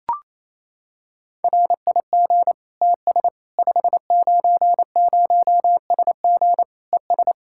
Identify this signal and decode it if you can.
Morse code, heard as a keyed tone, about 28 wpm, 700 Hz. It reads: RIG TS 590SG ES
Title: Practice QSO - Exchange 2 - Info - 28wpm